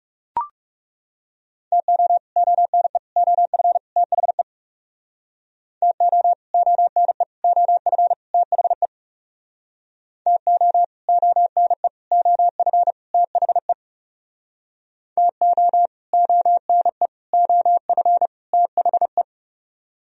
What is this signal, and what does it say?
Morse code, heard as a keyed tone, about 39 wpm, 700 Hz. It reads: TO ONE OF THE TO ONE OF THE TO ONE OF THE TO ONE OF THE